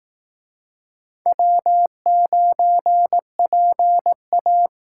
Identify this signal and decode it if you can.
Morse code, heard as a keyed tone, about 18 wpm, 700 Hz. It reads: W9PA
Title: Speed Racing - USA 1x2 Call Signs 15wpm